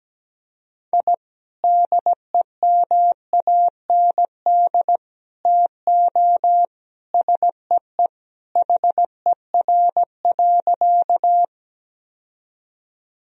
Morse code, heard as a keyed tone, about 17 wpm, 700 Hz. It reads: I DEMAND TO SEE HER.
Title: Sentences from Top 1000 Words - Review Part 4 - 17wpm